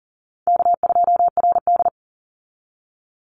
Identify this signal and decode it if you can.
Morse code, heard as a keyed tone, about 40 wpm, 700 Hz. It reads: K2RD
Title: Easy Call Signs Repeated 3x - 40wpm